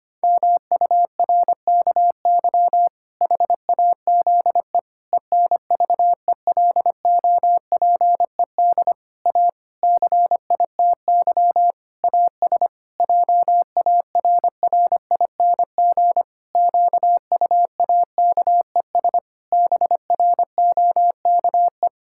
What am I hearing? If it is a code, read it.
Morse code, heard as a keyed tone, about 25 wpm, 700 Hz. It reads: MURKY HAZE ENVELOPED A CITY AS JARRING QUAKES BROKE
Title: Pangram Challenge v2 - 25wpm